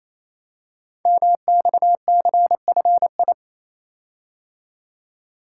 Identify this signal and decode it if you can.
Morse code, heard as a keyed tone, about 28 wpm, 700 Hz. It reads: MXCFI
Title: Code Groups - 5 Letters - 28wpm